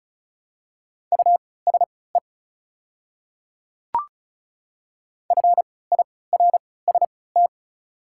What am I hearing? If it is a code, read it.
Morse code, heard as a keyed tone, about 36 wpm, spaced as for 20 wpm, 690 Hz. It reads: USE FIRST
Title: 3x Character Spacing - Top 100 Words - 35wpm